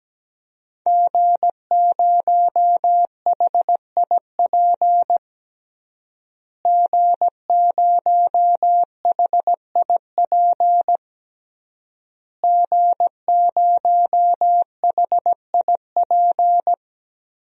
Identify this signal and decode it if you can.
Morse code, heard as a keyed tone, about 17 wpm, 700 Hz. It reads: G0HIP G0HIP G0HIP
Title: DXCC-100 Call Signs - 5 Characters - Repeated 3x - 17wpm